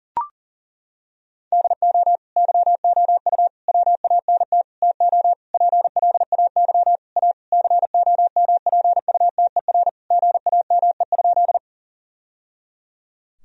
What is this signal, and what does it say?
Morse code, heard as a keyed tone, about 37 wpm, 700 Hz. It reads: DO YOU WANT TO PLAY A COMPUTER GAME?
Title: Sentences from Top 1000 Words - Review Part 3 - 40wpm